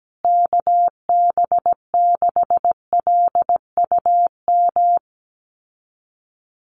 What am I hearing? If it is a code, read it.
Morse code, heard as a keyed tone, about 17 wpm, 700 Hz. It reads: KB6LUM
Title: USA Call Signs Repeated 2x 17wpm